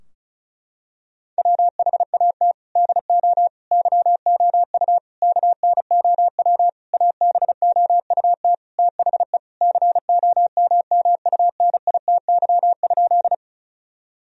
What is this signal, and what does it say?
Morse code, heard as a keyed tone, about 34 wpm, 700 Hz. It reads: WHAT DO YOU KNOW ABOUT THE COMMUNITY?
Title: Sentences from Top 1000 Words - Review Part 4 - 35wpm